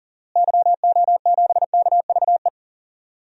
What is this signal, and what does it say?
Morse code, heard as a keyed tone, about 40 wpm, 700 Hz. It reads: YO7KVE
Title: DXCC-100 Call Signs - 6 Characters - Repeated 3x - 40wpm